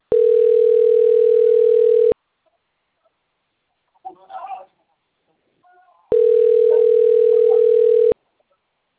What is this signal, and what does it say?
Ringback tone (North America), 440 + 480 Hz.
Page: Itatiba